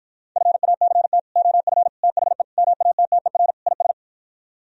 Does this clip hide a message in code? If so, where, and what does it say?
Yes: Morse code from 0.4 to 3.9 s, heard as a keyed tone, about 52 wpm, 700 Hz. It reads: FACT OF THE MATTER IS